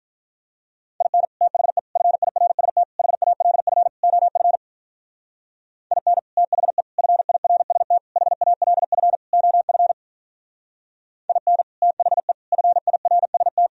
Morse code, heard as a keyed tone, about 46 wpm, 700 Hz. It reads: IN THE FIRST HALF OF IN THE FIRST HALF OF IN THE FIRST